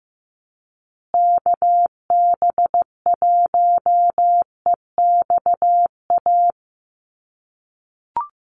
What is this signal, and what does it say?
Morse code, heard as a keyed tone, about 15 wpm, 700 Hz. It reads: KB1EXA
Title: USA Call Signs Repeated 3x 15wpm